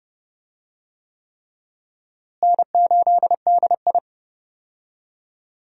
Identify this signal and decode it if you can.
Morse code, heard as a keyed tone, about 30 wpm, 700 Hz. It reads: N8DI
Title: USA Call Signs - 4 Characters - Repeated 3x - 30wpm